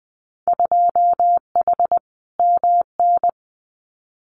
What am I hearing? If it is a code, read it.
Morse code, heard as a keyed tone, about 20 wpm, 700 Hz. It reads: 2H MN
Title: Winter Field Day Exchanges 20wpm